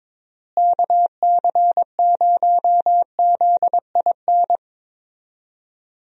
Morse code, heard as a keyed tone, about 22 wpm, 700 Hz. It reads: KC0ZIN
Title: USA Call Signs - 6 Characters - Repeated 3x - 22wpm